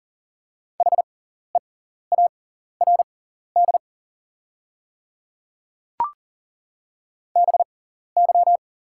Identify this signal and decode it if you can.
Morse code, heard as a keyed tone, about 40 wpm, spaced as for 14 wpm, 700 Hz. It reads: HEARD BY
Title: 6x Character Spacing - Top 500 Words - 40wpm